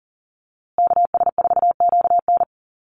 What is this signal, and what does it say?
Morse code, heard as a keyed tone, about 40 wpm, 700 Hz. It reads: KS4QN
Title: USA Call Signs Repeated 3x 40wpm